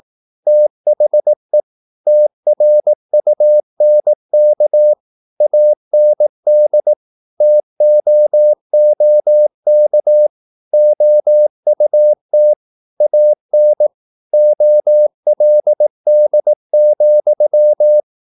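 Morse code, heard as a keyed tone, about 18 wpm, 600 Hz. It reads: THE TRUNK AND TOOK OUT AN OLD,
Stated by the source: Morse practice at 18 wpm, English text, from CW med Gnister.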